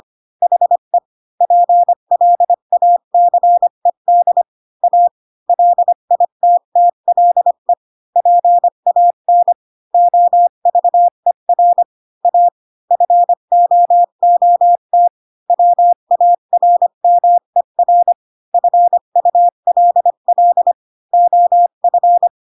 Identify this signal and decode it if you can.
Morse code, heard as a keyed tone, about 25 wpm, 700 Hz. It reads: HE PLACED A LITTLE PAN OVER A FOOT WARMER FULL OF
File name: Pinocchio / Pinocchio ch5 20 25 ik1hge